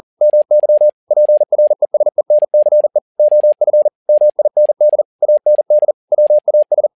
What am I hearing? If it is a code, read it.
Morse code, heard as a keyed tone, about 40 wpm, 600 Hz. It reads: MY PRESENCE OF MIND AND WAS
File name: learning center / tower trouble / tower touble 40wpm